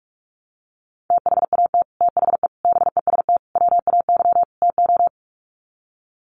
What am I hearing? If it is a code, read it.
Morse code, heard as a keyed tone, about 44 wpm, 700 Hz. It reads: THAT THE BEST WAY TO
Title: Sets of 5 Words Encore2 45wpm